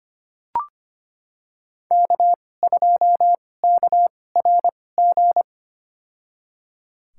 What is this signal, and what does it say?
Morse code, heard as a keyed tone, about 25 wpm, spaced as for 18 wpm, 700 Hz. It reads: K2KRG
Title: USA Call Signs - 5 Characters - 2x Character Spacing - 25wpm